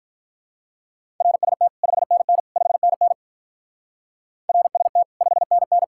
Morse code, heard as a keyed tone, about 46 wpm, 700 Hz. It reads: RST 5NN 5NN RST 5NN